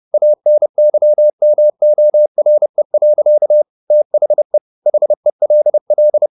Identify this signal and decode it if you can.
Morse code, heard as a keyed tone, about 30 wpm, 600 Hz. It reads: ANYMORE. THE HELL